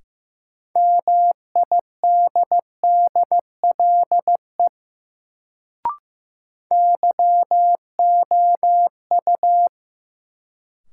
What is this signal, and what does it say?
Morse code, heard as a keyed tone, about 15 wpm, 700 Hz. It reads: MIDDLE YOU
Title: Sets of 2 Words - New Word Drill - 15wpm